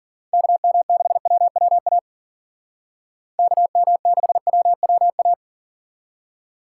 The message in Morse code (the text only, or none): KM6WWA KM6WWA